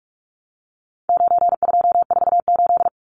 Morse code, heard as a keyed tone, about 44 wpm, 700 Hz. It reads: 9248
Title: Numbers - 4-Digit 45wpm